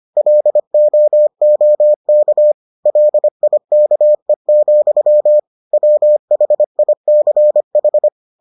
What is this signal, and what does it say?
Morse code, heard as a keyed tone, about 25 wpm, 600 Hz. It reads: LOOK LIKE, WHICH